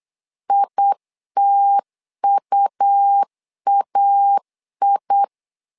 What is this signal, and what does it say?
Morse code, heard as a keyed tone, about 8 wpm, 800 Hz. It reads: ITUAI
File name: 4 Q 8wpm